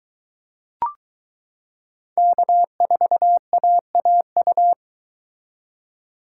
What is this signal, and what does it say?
Morse code, heard as a keyed tone, about 22 wpm, 700 Hz. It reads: K4AAU